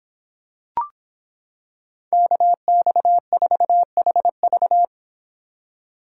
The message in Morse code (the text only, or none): KX4HV